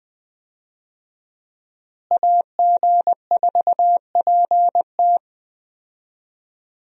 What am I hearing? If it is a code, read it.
Morse code, heard as a keyed tone, about 20 wpm, 700 Hz. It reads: AG4PT
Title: USA Call Signs - 5 Characters - Repeated 2x - 20wpm